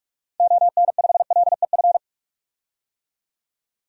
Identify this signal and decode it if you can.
Morse code, heard as a keyed tone, about 45 wpm, 690 Hz. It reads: ON5LEF
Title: DXCC-100 Call Signs - 6 Characters - 45wpm